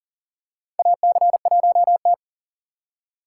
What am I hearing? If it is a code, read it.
Morse code, heard as a keyed tone, about 40 wpm, 700 Hz. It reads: AC1T
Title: USA Call Signs - 8x Character Spacing - 40wpm